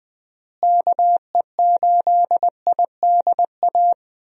Morse code, heard as a keyed tone, about 20 wpm, 700 Hz. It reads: KE8IDA
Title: USA Call Signs - 6 Characters - Repeated 3x - 20wpm